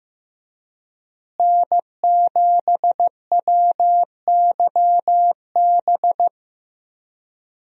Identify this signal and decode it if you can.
Morse code, heard as a keyed tone, about 15 wpm, 700 Hz. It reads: N7WYB
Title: USA Call Signs - 5 Characters - Repeated 3x - 15wpm